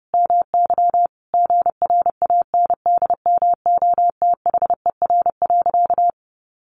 Morse code, heard as a keyed tone, about 28 wpm, 700 Hz. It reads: MY GRANDMOTHER.
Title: Sentences from Top 1200 Words 30wpm